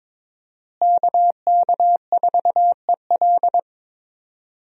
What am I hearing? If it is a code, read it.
Morse code, heard as a keyed tone, about 22 wpm, 690 Hz. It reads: KK4EL
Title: USA Call Signs - 5 Characters - Repeated 3x - 22wpm